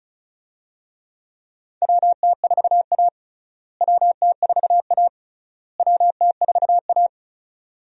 Morse code, heard as a keyed tone, about 34 wpm, 700 Hz. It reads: WT4A WT4A WT4A
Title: USA Call Signs - 4 Characters - Repeated 3x - 35wpm